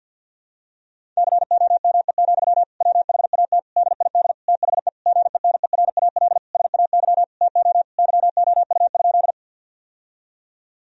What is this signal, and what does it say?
Morse code, heard as a keyed tone, about 52 wpm, 700 Hz. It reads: COME, WHAT DID THE GENERAL SAY TO YOU?
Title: Sentences from Top 1000 Words - Review Part 3 - 50wpm